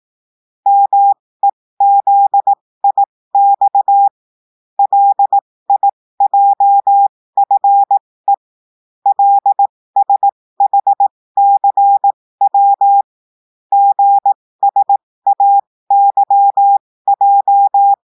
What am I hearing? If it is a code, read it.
Morse code, heard as a keyed tone, about 18 wpm, spaced as for 15 wpm, 800 Hz. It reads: MEZIX LIJFE LSHCW GSAYJ
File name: cours-du-lundi files / cours-1580-texte-chiffre-15-mpm